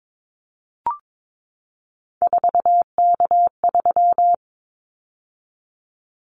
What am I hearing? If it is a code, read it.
Morse code, heard as a keyed tone, about 22 wpm, 700 Hz. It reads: 4K3
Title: Callsign Prefixes for Top 100 DXCC - 22wpm